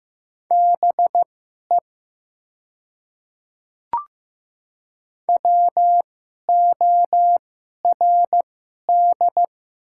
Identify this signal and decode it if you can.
Morse code, heard as a keyed tone, about 15 wpm, spaced as for 11 wpm, 700 Hz. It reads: BE WORD
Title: 2x Character Spacing - Top 100 Words - 15wpm